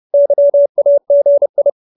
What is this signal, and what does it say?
Morse code, heard as a keyed tone, about 30 wpm, 570 Hz. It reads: YAGI
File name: QSO309 30